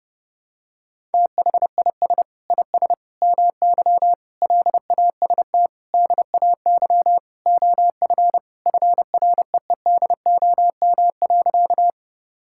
Morse code, heard as a keyed tone, about 30 wpm, 700 Hz. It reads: THIS IS MY LAST DAY OF FREEDOM.
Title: Sentences from Top 1000 Words - Review Part 5 - 30wpm